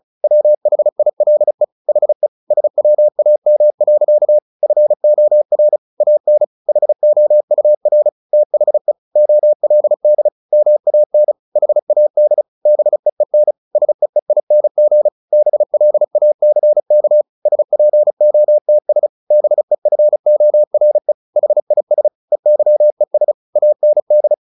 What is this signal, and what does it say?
Morse code, heard as a keyed tone, about 34 wpm, 600 Hz. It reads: WHILE HE SWAM. FOR AN HOUR THE OLD MAN HAD BEEN SEEING BLACK SPOTS BEFORE HIS EYES AND
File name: the old man and the sea 2 35wpm